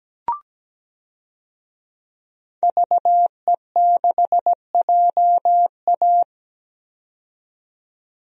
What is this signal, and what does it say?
Morse code, heard as a keyed tone, about 17 wpm, 700 Hz. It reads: VE6JA
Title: DXCC-100 Call Signs - 5 Characters - 17wpm